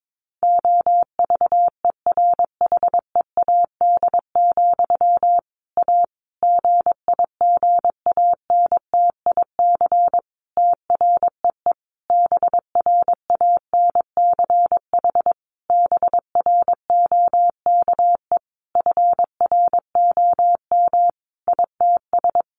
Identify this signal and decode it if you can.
Morse code, heard as a keyed tone, about 22 wpm, 700 Hz. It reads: OVERHEAD, A GIGANTIC TREE BRANCH BROKE FROM ITS
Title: A Sound Of Thunder 22wpm